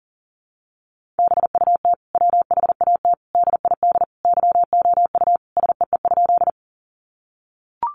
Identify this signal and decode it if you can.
Morse code, heard as a keyed tone, about 40 wpm, 700 Hz. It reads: BUT WHAT DID YOU SEE?
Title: Sentences from Top 100 Words 40wpm